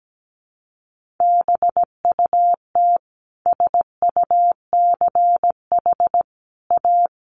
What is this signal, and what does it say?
Morse code, heard as a keyed tone, about 17 wpm, 700 Hz. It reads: BUT SUCH A
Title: Sentences from Top 1500 Words 17wpm